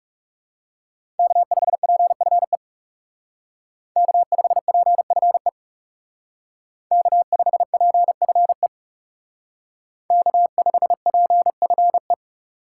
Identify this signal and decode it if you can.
Morse code, heard as a keyed tone, about 36 wpm, 700 Hz. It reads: K5PFE K5PFE K5PFE K5PFE
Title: Speed Racing - USA 1x3 Call Signs 30wpm